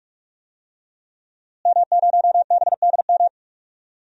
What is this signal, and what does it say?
Morse code, heard as a keyed tone, about 46 wpm, 700 Hz. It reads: M0BDM
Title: DXCC-100 Call Signs - 5 Characters - Repeated 3x - 45wpm